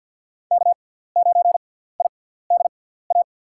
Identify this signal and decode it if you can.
Morse code, heard as a keyed tone, about 51 wpm, spaced as for 17 wpm, 700 Hz. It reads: K9IDA
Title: USA Call Signs - 5 Characters - 6x Character Spacing - 50wpm